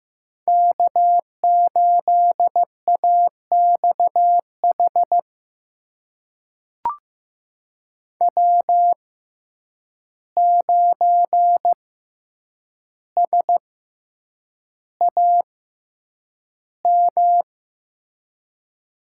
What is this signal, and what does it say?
Morse code, heard as a keyed tone, about 16 wpm, spaced as for 8 wpm, 700 Hz. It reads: K8AXH W9SAM
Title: USA Call Signs - 5 Characters - 6x Character Spacing - 15wpm